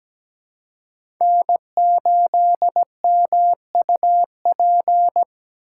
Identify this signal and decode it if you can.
Morse code, heard as a keyed tone, about 17 wpm, 700 Hz. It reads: N8MUP
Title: USA Call Signs - 5 Characters - 3x Character Spacing - 17wpm